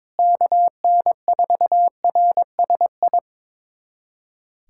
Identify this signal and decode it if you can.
Morse code, heard as a keyed tone, about 22 wpm, 700 Hz. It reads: KN4RSI
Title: USA Callsign DE Callsign - 6 Characters - 22wpm